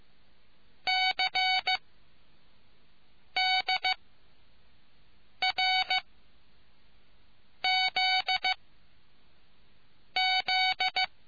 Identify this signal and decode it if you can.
Morse code, heard as a keyed tone, about 15 wpm, spaced as for 5 wpm, 750 Hz. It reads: CDRZZ